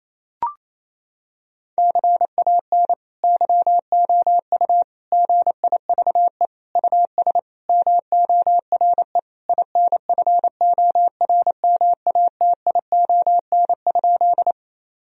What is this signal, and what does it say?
Morse code, heard as a keyed tone, about 27 wpm, 700 Hz. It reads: CAN YOU GIVE US MORE INFORMATION?
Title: Sentences from Top 1000 Words - Review Part 3 - 28wpm